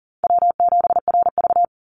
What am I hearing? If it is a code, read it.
Morse code, heard as a keyed tone, about 40 wpm, 700 Hz. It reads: W7RV
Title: NA SOTA Chaser Callsigns 40wpm